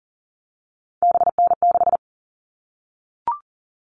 Morse code, heard as a keyed tone, about 40 wpm, 700 Hz. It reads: BN6
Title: All Cut Numbers - 3-Digit 40wpm